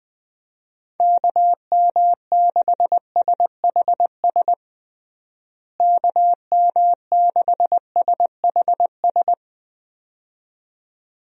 Morse code, heard as a keyed tone, about 21 wpm, 700 Hz. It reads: KM6SHS KM6SHS
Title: USA Call Signs - 6 Characters - Repeated 2x - 20wpm